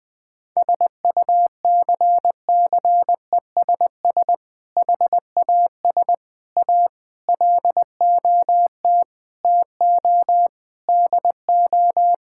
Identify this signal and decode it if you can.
Morse code, heard as a keyed tone, about 20 wpm, 700 Hz. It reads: SUCCESS HAS A LOT TO DO
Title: Sentences with 6-Word Phrases - Spoken Beginning - 20wpm